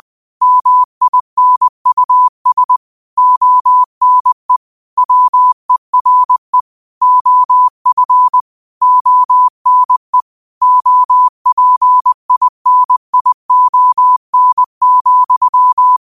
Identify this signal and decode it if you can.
Morse code, heard as a keyed tone, about 20 wpm, 1000 Hz. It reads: MINUS ONE WERE OF ONE OPINION,